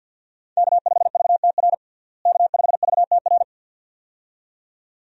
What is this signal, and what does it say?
Morse code, heard as a keyed tone, about 52 wpm, 700 Hz. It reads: K5VTR K5VTR
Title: USA Call Signs - 5 Characters - Repeated 2x - 50wpm